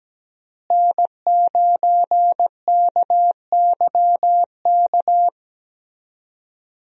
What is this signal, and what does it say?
Morse code, heard as a keyed tone, about 17 wpm, 700 Hz. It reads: N9KYK
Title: USA Call Signs - 5 Characters - Repeated 2x - 17wpm